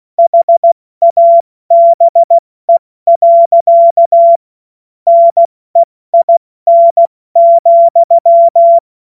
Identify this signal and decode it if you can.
Morse code, heard as a keyed tone, about 16 wpm, 680 Hz. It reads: HABE. NEIN,
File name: Morsekurs / Witze / Witz 27. Januar 2026 70 F80